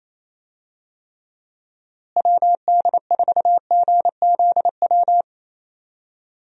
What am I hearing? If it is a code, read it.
Morse code, heard as a keyed tone, about 28 wpm, 700 Hz. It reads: WD4GZW
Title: USA Call Signs - 6 Characters - Repeated 3x - 28wpm